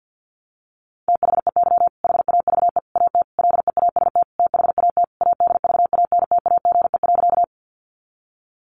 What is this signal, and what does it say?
Morse code, heard as a keyed tone, about 50 wpm, 700 Hz. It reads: THEY HAVE AT LEAST THAT ADVANTAGE.
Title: Sentences from Top 900 Words 50wpm